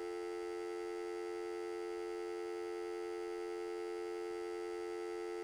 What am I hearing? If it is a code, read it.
Telephone call progress tone (North America), heard as a telephone network's tone, about 350, 440 Hz, with noise. Dial tone.